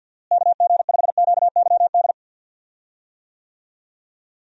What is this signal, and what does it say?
Morse code, heard as a keyed tone, about 53 wpm, 700 Hz. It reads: KG5QYD
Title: USA Call Signs - 6 Characters - Repeated 3x - 50wpm